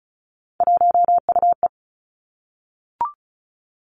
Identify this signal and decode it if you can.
Morse code, heard as a keyed tone, about 36 wpm, 700 Hz. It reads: 1UE
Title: Three Challenging Letter-Number Combinations 35wpm